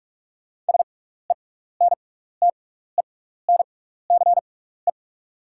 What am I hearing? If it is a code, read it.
Morse code, heard as a keyed tone, about 43 wpm, spaced as for 15 wpm, 690 Hz. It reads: SENTENCE
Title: 6x Character Spacing - Top 500 Words - 45wpm